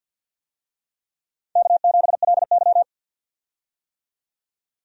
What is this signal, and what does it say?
Morse code, heard as a keyed tone, about 51 wpm, 700 Hz. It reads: K7LY
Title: USA Call Signs - 4 Characters - 50wpm